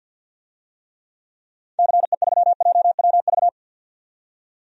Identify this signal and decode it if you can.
Morse code, heard as a keyed tone, about 51 wpm, 700 Hz. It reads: CE3JWV
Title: DXCC-100 Call Signs - 6 Characters - 50wpm